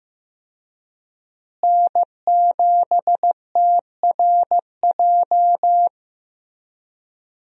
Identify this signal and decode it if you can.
Morse code, heard as a keyed tone, about 15 wpm, 700 Hz. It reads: N7TRJ
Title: USA Call Signs - 5 Characters - Repeated 3x - 15wpm